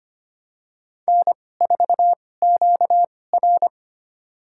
Morse code, heard as a keyed tone, about 25 wpm, spaced as for 18 wpm, 700 Hz. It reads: N4QR
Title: USA Call Signs - 2x Character Spacing - 25wpm